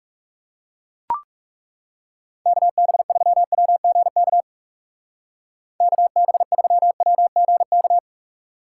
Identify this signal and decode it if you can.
Morse code, heard as a keyed tone, about 37 wpm, 700 Hz. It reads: KB3WGK KB3WGK
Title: Speed Racing - USA 2x3 Call Signs 30wpm